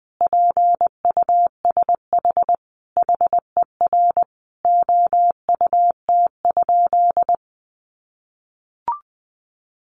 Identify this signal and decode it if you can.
Morse code, heard as a keyed tone, about 20 wpm, 700 Hz. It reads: PUSH HER OUT?
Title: Sentences from Top 1300 Words 20wpm